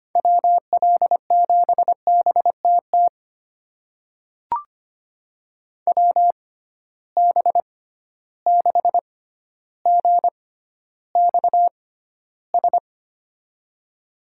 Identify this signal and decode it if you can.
Morse code, heard as a keyed tone, about 25 wpm, spaced as for 13 wpm, 700 Hz. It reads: WL7BTT WB6GXS
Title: USA Call Signs - 6 Characters - 6x Character Spacing - 25wpm